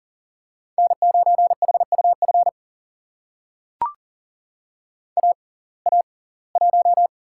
Morse code, heard as a keyed tone, about 39 wpm, spaced as for 24 wpm, 700 Hz. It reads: N9HUF AA1
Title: USA Call Signs - 5 Characters - 6x Character Spacing - 40wpm